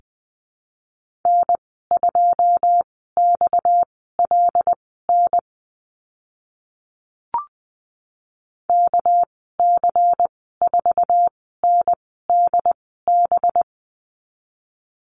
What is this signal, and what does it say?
Morse code, heard as a keyed tone, about 20 wpm, spaced as for 14 wpm, 700 Hz. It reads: N2XLN KC4NDB
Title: 231 Lesson 37g - Letter x - Call Signs 20wpm